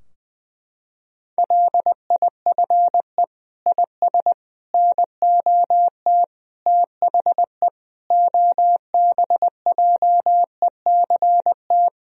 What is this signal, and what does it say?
Morse code, heard as a keyed tone, about 20 wpm, 700 Hz. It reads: LIFE IS NOT THE OBJECT
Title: Sentences from Top 1000 Words - Review Part 3 - 20wpm